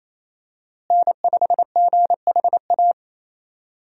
Morse code, heard as a keyed tone, about 28 wpm, 700 Hz. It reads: N5GHA
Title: USA Call Signs - 5 Characters - Repeated 2x - 28wpm